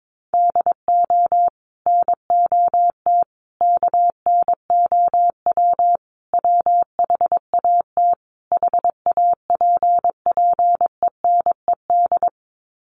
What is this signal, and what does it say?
Morse code, heard as a keyed tone, about 22 wpm, 700 Hz. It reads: DO NOT KNOW WHAT HAPPENED
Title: Sets of 5 Words Encore2 22wpm